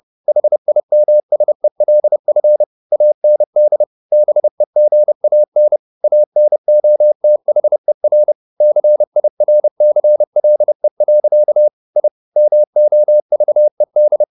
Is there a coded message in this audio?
Yes: Morse code from 0.3 to 14.2 s, heard as a keyed tone, about 30 wpm, 600 Hz. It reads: HIMSELF AND BEGAN ANOTHER CIRCLE. I MOVED